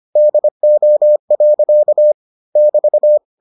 Morse code, heard as a keyed tone, about 25 wpm, 600 Hz. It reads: DO. <BT>